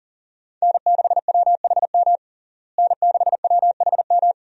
Morse code, heard as a keyed tone, about 40 wpm, 700 Hz. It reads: N6WHM N6WHM
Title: USA Call Signs - 5 Characters - Repeated 2x - 40wpm